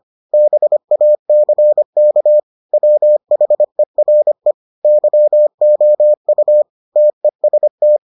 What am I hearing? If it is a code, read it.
Morse code, heard as a keyed tone, about 25 wpm, 600 Hz. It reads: BACK WHERE YOU TEST